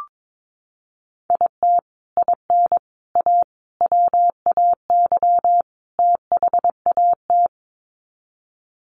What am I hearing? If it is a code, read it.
Morse code, heard as a keyed tone, about 22 wpm, 700 Hz. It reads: IT IN A WAY THAT
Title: Sets of 5 Words Encore2 22wpm